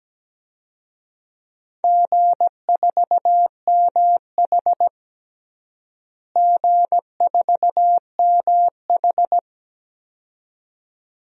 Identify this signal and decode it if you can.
Morse code, heard as a keyed tone, about 17 wpm, 700 Hz. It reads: G4MH G4MH
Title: DXCC-100 Call Signs - 4 Characters - Repeated 2x - 17wpm